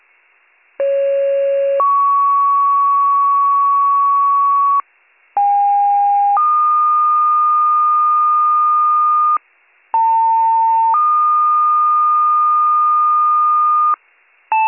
Two-tone page, heard as a one-second tone, then a three-second tone, about 550 then 1120 Hz, about 810 then 1220 Hz, about 900 then 1220 Hz. It is heard from 0.8 to 14.0 s.